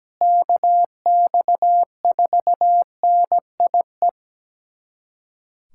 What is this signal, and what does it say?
Morse code, heard as a keyed tone, about 17 wpm, 700 Hz. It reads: KX4NIE
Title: USA Callsign DE Callsign - 6 Characters - 17wpm